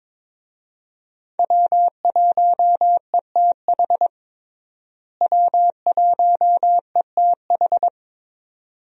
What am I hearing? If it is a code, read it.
Morse code, heard as a keyed tone, about 23 wpm, 700 Hz. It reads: W1ETH W1ETH
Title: USA Call Signs - 5 Characters - Repeated 2x - 22wpm